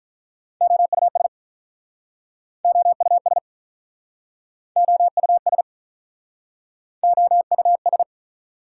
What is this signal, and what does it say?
Morse code, heard as a keyed tone, about 48 wpm, 690 Hz. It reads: OUS OUS OUS OUS